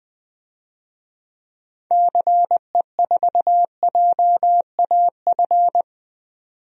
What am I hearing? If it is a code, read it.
Morse code, heard as a keyed tone, about 21 wpm, 700 Hz. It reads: CE4JAF
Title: DXCC-100 Call Signs - 6 Characters - 20wpm